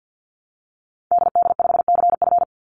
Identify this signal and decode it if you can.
Morse code, heard as a keyed tone, about 48 wpm, 700 Hz. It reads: DD5CF
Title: Call Signs Repeated 3x - 50wpm